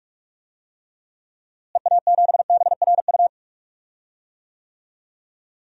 Morse code, heard as a keyed tone, about 47 wpm, 700 Hz. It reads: EA7BRU
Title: DXCC-100 Call Signs - 6 Characters - Repeated 3x - 45wpm